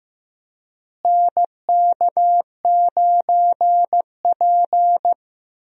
Morse code, heard as a keyed tone, about 15 wpm, 700 Hz. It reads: NK9P
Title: USA Call Signs - 4 Characters - Repeated 2x - 15wpm